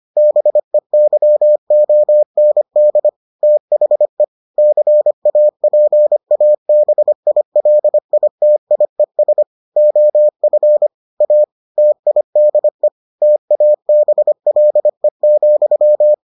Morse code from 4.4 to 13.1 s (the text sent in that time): CAPABILITIES OF A TIDE